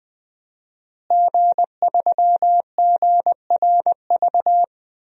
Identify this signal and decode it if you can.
Morse code, heard as a keyed tone, about 20 wpm, 700 Hz. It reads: G3GRV
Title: DXCC-100 Call Signs - 5 Characters - Repeated 3x - 20wpm